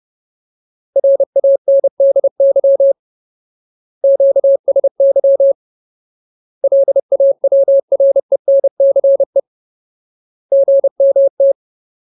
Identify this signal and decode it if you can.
Morse code, heard as a keyed tone, about 30 wpm, 550 Hz. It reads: RANDY QSY LAWRENCE GMT
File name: Ultimate Word List / 30WPM Ultimate